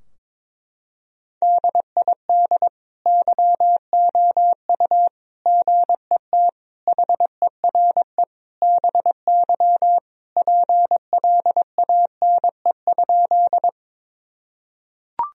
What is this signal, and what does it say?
Morse code, heard as a keyed tone, about 22 wpm, 700 Hz. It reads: DID YOU GET HERE BY PLANE?
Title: Sentences from Top 1000 Words - Review Part 3 - 22wpm